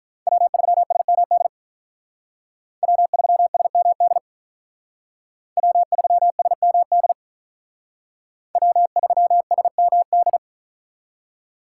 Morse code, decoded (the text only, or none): W3SMD W3SMD W3SMD W3SMD